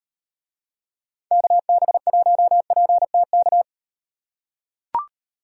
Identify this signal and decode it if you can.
Morse code, heard as a keyed tone, about 37 wpm, 700 Hz. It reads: KB1PTK